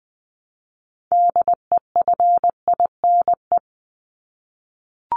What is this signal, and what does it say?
Morse code, heard as a keyed tone, about 20 wpm, 700 Hz. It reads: DEFINE
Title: ED Suffix - Top 100 Words with Base Words 20wpm